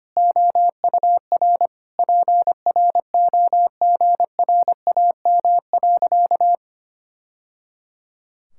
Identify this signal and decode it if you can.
Morse code, heard as a keyed tone, about 25 wpm, 700 Hz. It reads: OUR PROGRAM.
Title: Sentences from Top 1000 Words - Review Part 5 - 25wpm